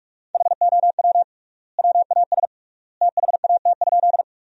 Morse code, heard as a keyed tone, about 46 wpm, 700 Hz. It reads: HOW WAS THAT?
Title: 3x Word-Spacing - Sentences from Top 100 Words - 45wpm